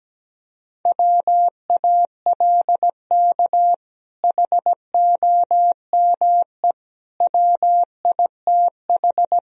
Morse code, heard as a keyed tone, about 17 wpm, 700 Hz. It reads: WALK HOME WITH
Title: Sentences from Top 1000 Words - Review Part 1 - 17wpm